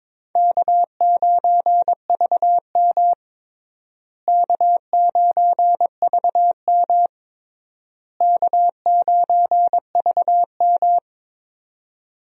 Morse code, heard as a keyed tone, about 22 wpm, 700 Hz. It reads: K9VM K9VM K9VM
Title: USA Call Signs - 4 Characters - Repeated 3x - 22wpm